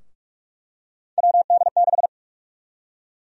Morse code, heard as a keyed tone, about 41 wpm, 700 Hz. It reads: WD6
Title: USA Call Sign Prefixes - 45wpm